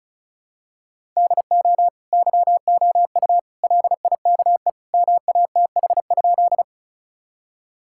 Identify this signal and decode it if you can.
Morse code, heard as a keyed tone, about 34 wpm, 700 Hz. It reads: DO YOU LIKE MATH?
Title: Sentences from Top 1000 Words - Review Part 6 - 35wpm